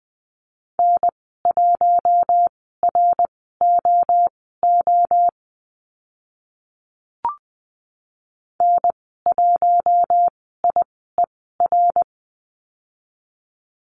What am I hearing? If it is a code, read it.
Morse code, heard as a keyed tone, about 20 wpm, spaced as for 14 wpm, 700 Hz. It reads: N1ROO N1IER
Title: 055 Lesson 10g - Letter r - Call Signs 20wpm